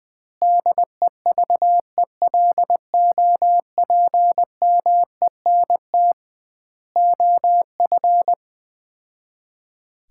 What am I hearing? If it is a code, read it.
Morse code, heard as a keyed tone, about 21 wpm, 700 Hz. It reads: DEVELOPMENT OF
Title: 2x Word-Spacing - Sets of 3 Words - 20wpm